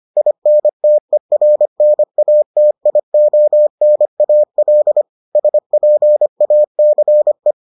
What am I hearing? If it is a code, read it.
Morse code, heard as a keyed tone, about 25 wpm, 600 Hz. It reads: INTERNATIONAL SPACE